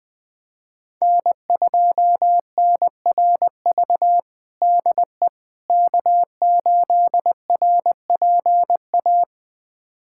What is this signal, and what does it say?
Morse code, heard as a keyed tone, about 19 wpm, 700 Hz. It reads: N2NRV DE K8RPA
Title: USA Callsign DE Callsign - 5 Characters - 20wpm